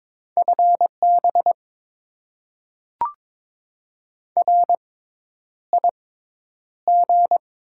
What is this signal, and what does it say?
Morse code, heard as a keyed tone, about 21 wpm, spaced as for 10 wpm, 700 Hz. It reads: FB RIG
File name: Top 20 QSO Elements - 6x Character Spacing - 22wpm